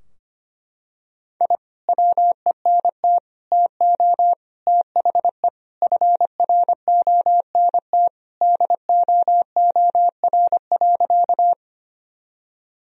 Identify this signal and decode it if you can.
Morse code, heard as a keyed tone, about 25 wpm, 700 Hz. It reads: I WENT TO THE FRONT DOOR.
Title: Sentences from Top 1000 Words - Review Part 2 - 25wpm